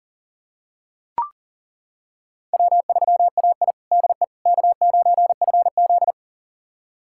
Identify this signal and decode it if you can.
Morse code, heard as a keyed tone, about 40 wpm, 690 Hz. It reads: W3AI DE K9FZ
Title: USA Callsign DE Callsign - 4 Characters - 40wpm